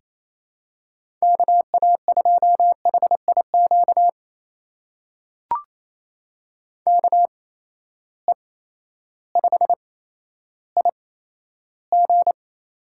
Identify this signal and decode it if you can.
Morse code, heard as a keyed tone, about 27 wpm, spaced as for 13 wpm, 700 Hz. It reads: KA2HIQ KE5IG
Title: USA Call Signs - 6 Characters - 8x Character Spacing - 28wpm